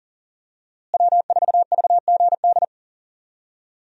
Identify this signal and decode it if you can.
Morse code, heard as a keyed tone, about 37 wpm, 700 Hz. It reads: W4VGD
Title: USA Call Signs - 5 Characters - Repeated 3x - 40wpm